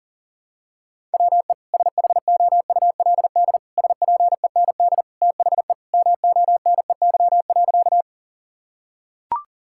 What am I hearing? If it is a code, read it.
Morse code, heard as a keyed tone, about 40 wpm, 700 Hz. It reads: WE SHOULD SPEND THE MONEY.
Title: Sentences from Top 1000 Words - Review Part 5 - 40wpm